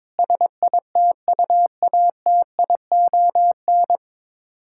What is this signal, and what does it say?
Morse code, heard as a keyed tone, about 22 wpm, 700 Hz. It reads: SITUATION
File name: Sets of 9 Words - 22wpm